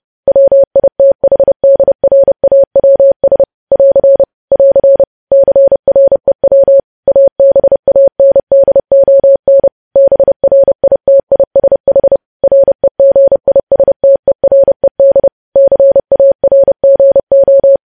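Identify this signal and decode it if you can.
Morse code, heard as a keyed tone, about 30 wpm, 550 Hz. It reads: WITHDRAWS <AR> <AR> CREW ABANDON BRITISH REGISTERED CARGO
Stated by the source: Morse code practice of news headlines sent at 30 wpm.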